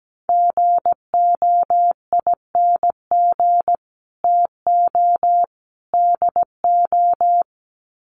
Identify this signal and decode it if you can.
Morse code, heard as a keyed tone, about 17 wpm, 700 Hz. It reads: GOING TO DO
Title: Sets of 3 Words 17wpm